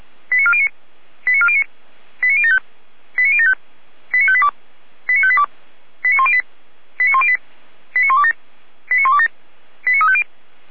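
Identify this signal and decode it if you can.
Five-tone selective call (ZVEI), heard as quick runs of five tones, tones of about 70 ms. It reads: 89409 89409 89075 89075 89582 89582 89108 89108 89127 89127 89350
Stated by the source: ZVEI five-tone selective call received by SDR.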